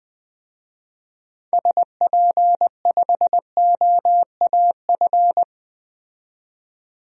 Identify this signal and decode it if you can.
Morse code, heard as a keyed tone, about 20 wpm, 700 Hz. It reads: SP5OAF
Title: DXCC-100 Call Signs - 6 Characters - 20wpm